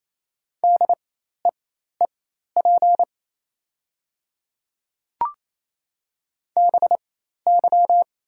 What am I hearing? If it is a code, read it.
Morse code, heard as a keyed tone, about 27 wpm, spaced as for 13 wpm, 690 Hz. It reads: DEEP BY
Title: 4x Character Spacing - Top 500 Words - 28wpm